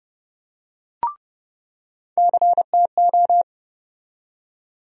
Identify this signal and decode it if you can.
Morse code, heard as a keyed tone, about 30 wpm, 700 Hz. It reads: CTO
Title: USA Call Sign Postfixes - 3 Letters 30wpm